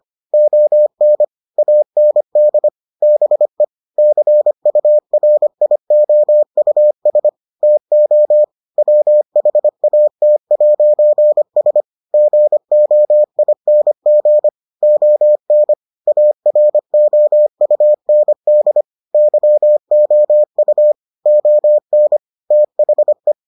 Morse code, heard as a keyed tone, about 25 wpm, 600 Hz. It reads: ON AND BE CURIOUS TO WHAT'S GOING ON AROUND YOU ON THE